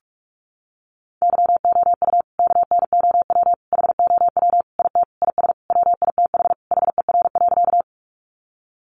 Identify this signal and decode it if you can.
Morse code, heard as a keyed tone, about 44 wpm, 700 Hz. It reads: YOU KNOW HOW IT IS WITH HER.
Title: Sentences from Top 100 Words 45wpm